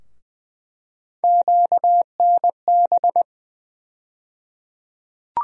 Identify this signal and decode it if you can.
Morse code, heard as a keyed tone, about 20 wpm, 700 Hz. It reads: QNB
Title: USA Call Sign Postfixes - 3 Letters 20wpm